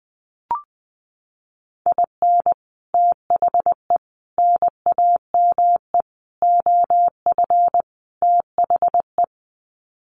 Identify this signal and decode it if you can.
Morse code, heard as a keyed tone, about 20 wpm, 700 Hz. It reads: IN THE NAME OF THE
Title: Sets of 5 Words Encore 20wpm